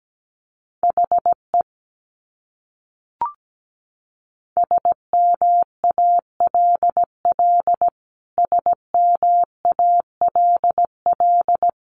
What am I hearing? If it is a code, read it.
Morse code, heard as a keyed tone, about 17 wpm, 700 Hz. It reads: HE SMALL SMALL
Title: Top 300 Words Repeated 17wpm